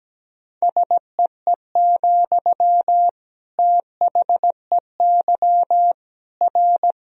Morse code, heard as a keyed tone, about 17 wpm, 700 Hz. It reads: SEE, THEY R
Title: Sentences from Top 200 Words and Abbreviations 17wpm